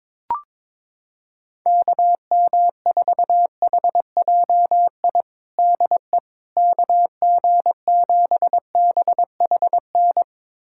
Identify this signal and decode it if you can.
Morse code, heard as a keyed tone, about 22 wpm, 700 Hz. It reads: KM4HJI DE KG7BHN
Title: USA Callsign DE Callsign - 6 Characters - 22wpm